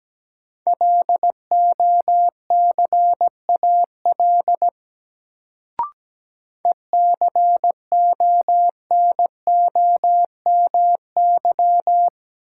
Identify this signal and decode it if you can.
Morse code, heard as a keyed tone, about 17 wpm, 700 Hz. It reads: LOCAL ECONOMY
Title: Sentences from Top 1000 Words - Review Part 3 - New Word Drill - 17wpm